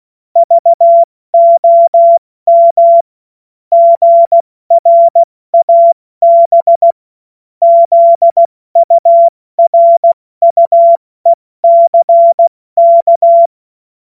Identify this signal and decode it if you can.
Morse code, heard as a keyed tone, about 16 wpm, 680 Hz. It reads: VOM GRAB ZURUECK
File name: Morsekurs / Witze / Witz 23. August 2022 70 F80